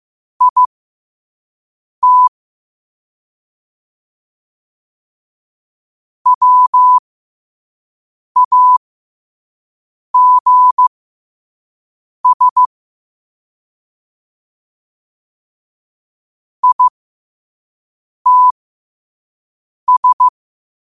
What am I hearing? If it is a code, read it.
Morse code, heard as a keyed tone, about 15 wpm, spaced as for 5 wpm, 1000 Hz. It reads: IT WAGS ITS
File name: 5wpm